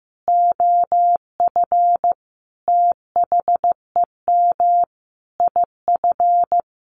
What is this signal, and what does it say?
Morse code, heard as a keyed tone, about 15 wpm, 700 Hz. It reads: OF THEM IF
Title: Sentences from Top 1100 Words 15wpm